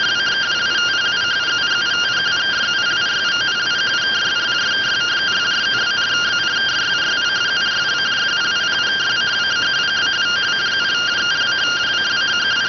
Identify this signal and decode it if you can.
Two-tone teleprinter signal, about 170 Hz shift.